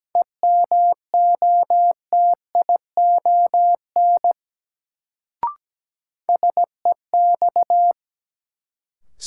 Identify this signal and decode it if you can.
Morse code, heard as a keyed tone, about 17 wpm, 700 Hz. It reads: EMOTION SEX
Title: Sentences from Top 1600 Words - New Word Drill - 17wpm